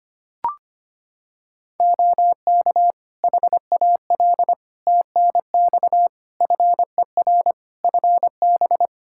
Morse code, heard as a keyed tone, about 25 wpm, 700 Hz. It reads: OK HAL TNX FER FB
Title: Practice QSO - Exchange 4 - Ending - 25wpm